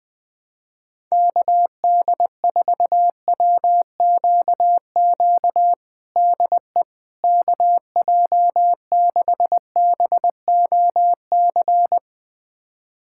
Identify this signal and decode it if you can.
Morse code, heard as a keyed tone, about 20 wpm, 700 Hz. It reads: KD4WQQ DE KJ6BOC
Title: USA Callsign DE Callsign - 6 Characters - 20wpm